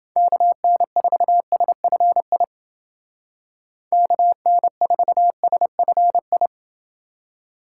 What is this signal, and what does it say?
Morse code, heard as a keyed tone, about 28 wpm, 700 Hz. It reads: KN4SFI KN4SFI